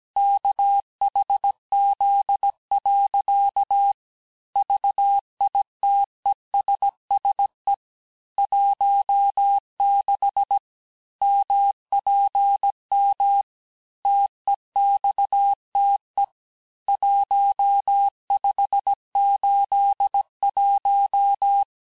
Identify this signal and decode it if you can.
Morse code, heard as a keyed tone, about 17 wpm, 800 Hz. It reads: KHZ. VITESSE 16 MPM TEXTE 1581
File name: cours-du-lundi files / corrige-cours-1581-texte-chiffre-vitesse-16-mpm